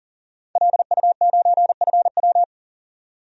Morse code, heard as a keyed tone, about 40 wpm, 700 Hz. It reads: LU9FW
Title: DXCC-100 Call Signs - 5 Characters - 40wpm